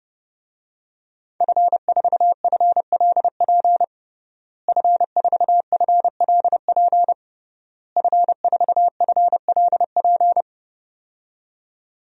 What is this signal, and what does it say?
Morse code, heard as a keyed tone, about 30 wpm, 700 Hz. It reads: F4FLP F4FLP F4FLP
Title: DXCC-100 Call Signs - 5 Characters - Repeated 3x - 30wpm